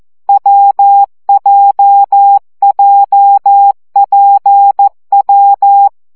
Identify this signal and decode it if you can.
Morse code, heard as a keyed tone, about 14 wpm, 800 Hz. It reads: WJJPW
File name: CW gyakorlas w p j 12 baud